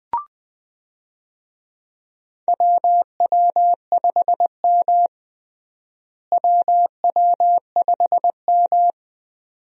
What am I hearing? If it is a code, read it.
Morse code, heard as a keyed tone, about 20 wpm, 700 Hz. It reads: WW5M WW5M
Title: USA Call Signs - 4 Characters - Repeated 2x - 20wpm